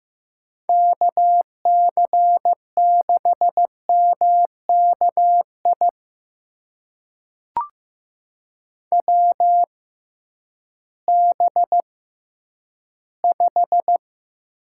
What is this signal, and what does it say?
Morse code, heard as a keyed tone, about 15 wpm, spaced as for 10 wpm, 700 Hz. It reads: KC6MKI WB5
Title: USA Call Signs - 6 Characters - 6x Character Spacing - 15wpm